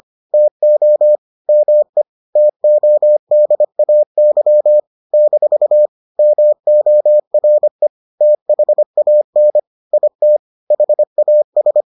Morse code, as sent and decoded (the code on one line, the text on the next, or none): - --- / -- . / - --- -.. .- -.-- / -....- / -- --- .-. / - .... .- -. / .. - / .... .- ...
TO ME TODAY - MORE THAN IT HAS